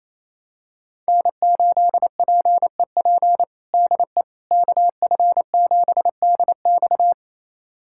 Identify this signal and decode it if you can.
Morse code, heard as a keyed tone, about 28 wpm, 700 Hz. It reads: N8PEP DE KF7DX
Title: USA Callsign DE Callsign - 5 Characters - 28wpm